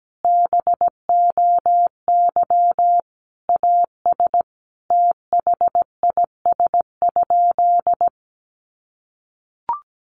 Morse code, heard as a keyed tone, about 17 wpm, 700 Hz. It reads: BOY AS THIS?
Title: Sentences from Top 400 Words 17wpm